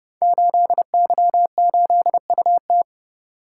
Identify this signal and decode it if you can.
Morse code, heard as a keyed tone, about 30 wpm, 700 Hz. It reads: 8Y8UT
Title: Code Groups - 5 Letter:Numbers - 30wpm